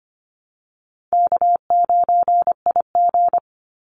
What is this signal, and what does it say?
Morse code, heard as a keyed tone, about 25 wpm, 700 Hz. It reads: K9IG
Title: Easy Call Signs - 25wpm